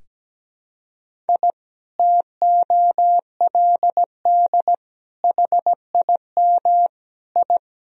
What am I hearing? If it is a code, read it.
Morse code, heard as a keyed tone, about 17 wpm, 700 Hz. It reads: I TOLD HIM I
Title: Sentences from Top 1000 Words - Review Part 1 - 17wpm